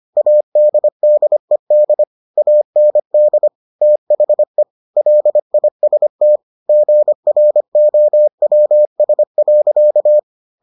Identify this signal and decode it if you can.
Morse code, heard as a keyed tone, about 25 wpm, 600 Hz. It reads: ADDED AND THE LIST GROWS.